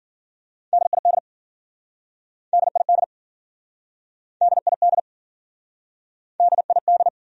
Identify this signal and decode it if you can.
Morse code, heard as a keyed tone, about 53 wpm, 700 Hz. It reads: DID DID DID DID